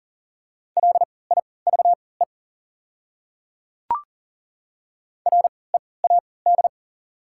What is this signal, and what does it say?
Morse code, heard as a keyed tone, about 40 wpm, spaced as for 23 wpm, 690 Hz. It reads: LIVE READ
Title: 3x Character Spacing - Top 300 Words - 40wpm